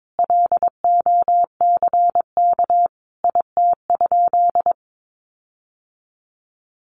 Morse code, heard as a keyed tone, about 22 wpm, 700 Hz. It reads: LOCK IT?
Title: Sentences from Top 1400 Words 22wpm